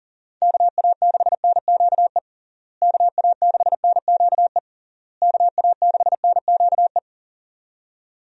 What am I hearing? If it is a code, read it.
Morse code, heard as a keyed tone, about 40 wpm, 700 Hz. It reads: KA6NQE KA6NQE KA6NQE
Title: USA Call Signs - 6 Characters - Repeated 3x - 40wpm